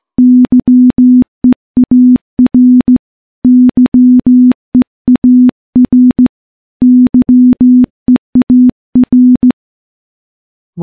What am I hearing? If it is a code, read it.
Morse code, heard as a keyed tone, about 15 wpm, 250 Hz. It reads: YEAR YEAR YEAR
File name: morse-training-15WPM